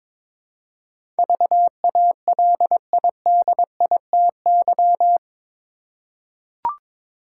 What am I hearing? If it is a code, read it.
Morse code, heard as a keyed tone, about 22 wpm, 700 Hz. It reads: VALIDITY
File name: practice / 22wpm / Eight Letter Words - 22wpm